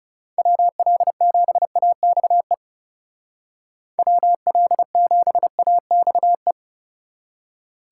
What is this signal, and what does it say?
Morse code, heard as a keyed tone, about 37 wpm, 700 Hz. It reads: WL7AXE WL7AXE
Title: Speed Racing - USA 2x3 Call Signs 30wpm